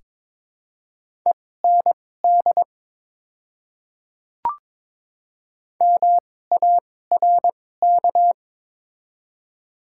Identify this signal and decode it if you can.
Morse code, heard as a keyed tone, about 22 wpm, spaced as for 16 wpm, 700 Hz. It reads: END MARK
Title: 2x Character Spacing - Top 300 Words - 22wpm